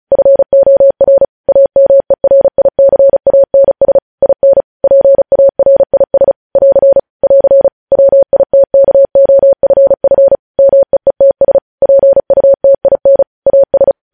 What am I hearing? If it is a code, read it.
Morse code, heard as a keyed tone, about 36 wpm, 550 Hz. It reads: FOR AMERICANS IN PARIS <AR> <AR> WITKOFF MEETS PUTIN AS
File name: podcast / News headlines-1x-35x35